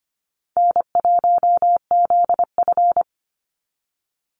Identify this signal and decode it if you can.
Morse code, heard as a keyed tone, about 25 wpm, 700 Hz. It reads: N1ZF
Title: NA SOTA Activator Callsigns 25wpm